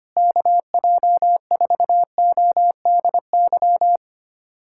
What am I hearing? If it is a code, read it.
Morse code, heard as a keyed tone, about 25 wpm, 700 Hz. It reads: KJ4ODY
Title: USA Callsign DE Callsign 25wpm